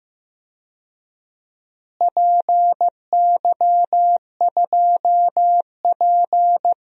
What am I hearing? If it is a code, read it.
Morse code, heard as a keyed tone, about 15 wpm, 700 Hz. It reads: PY2P
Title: DXCC-100 Call Signs - 4 Characters - Repeated 3x - 15wpm